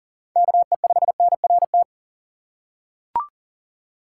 Morse code, heard as a keyed tone, about 40 wpm, 700 Hz. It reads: KE5NRT
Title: USA Call Signs - 2x Character Spacing - 40wpm